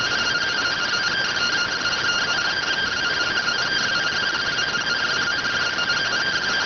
Two-tone teleprinter signal, about 170 Hz shift.